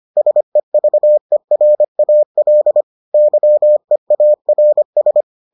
Morse code, heard as a keyed tone, about 25 wpm, 600 Hz. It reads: SEVERAL YEARS